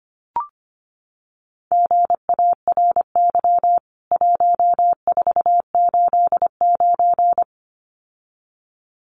Morse code, heard as a keyed tone, about 25 wpm, 700 Hz. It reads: GARY 1489
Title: CWT Exchange 25wpm